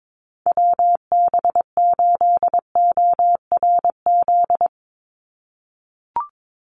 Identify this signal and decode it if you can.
Morse code, heard as a keyed tone, about 22 wpm, 700 Hz. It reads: WB8ORZ
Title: USA Call Signs 22wpm